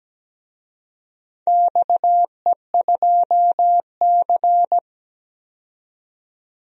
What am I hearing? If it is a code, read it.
Morse code, heard as a keyed tone, about 17 wpm, 700 Hz. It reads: XE2C
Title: DXCC-100 Call Signs - 4 Characters - 17wpm